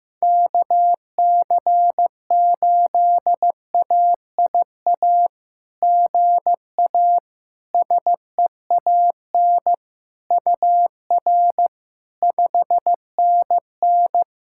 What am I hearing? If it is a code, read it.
Morse code, heard as a keyed tone, about 15 wpm, 690 Hz. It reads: KC8AIA GA SEAN UR 5NN
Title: SOTA QSO Practice 15wpm